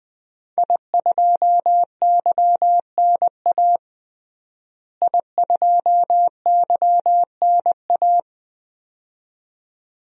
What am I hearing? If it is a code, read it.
Morse code, heard as a keyed tone, about 20 wpm, 700 Hz. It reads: I2YNA I2YNA
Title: DXCC-100 Call Signs - 5 Characters - Repeated 2x - 20wpm